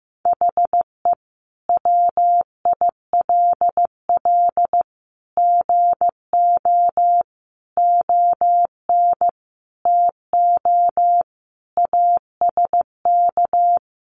Morse code, heard as a keyed tone, about 15 wpm, 700 Hz. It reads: HE WILL GO ON TO ASK
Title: Sentences from Top 1500 Words 15wpm